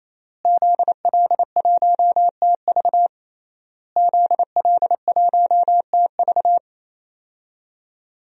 Morse code, heard as a keyed tone, about 27 wpm, 700 Hz. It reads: ZL1TV ZL1TV
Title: DXCC-100 Call Signs - 5 Characters - Repeated 3x - 28wpm